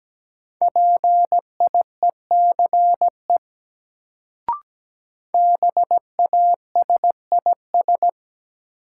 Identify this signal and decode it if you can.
Morse code, heard as a keyed tone, about 17 wpm, 700 Hz. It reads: PIECE BASIS
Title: Sets of 3 Words - New Word Drill - 17wpm